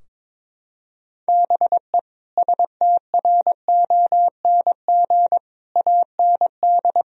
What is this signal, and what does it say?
Morse code, heard as a keyed tone, about 22 wpm, 700 Hz. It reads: BE STRONG AND
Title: Sentences from Top 1000 Words - Review Part 3 - 22wpm